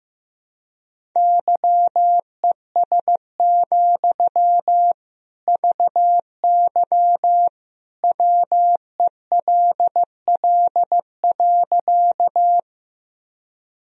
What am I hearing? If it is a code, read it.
Morse code, heard as a keyed tone, about 16 wpm, 700 Hz. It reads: YES, VY WELL.
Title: Sentences from Top 500 Words and Abbreviations 15wpm